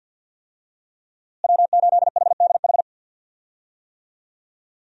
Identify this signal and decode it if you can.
Morse code, heard as a keyed tone, about 52 wpm, 700 Hz. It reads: W8HDH
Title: USA Call Signs - 5 Characters - Repeated 2x - 50wpm